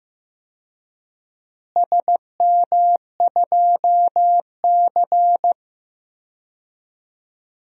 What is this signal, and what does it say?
Morse code, heard as a keyed tone, about 15 wpm, 700 Hz. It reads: SM2C
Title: DXCC-100 Call Signs - 4 Characters - Repeated 2x - 15wpm